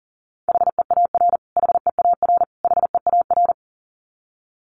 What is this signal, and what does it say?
Morse code, heard as a keyed tone, about 40 wpm, 700 Hz. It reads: HEAR HEAR HEAR
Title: Top 500 Words Repeated 40wpm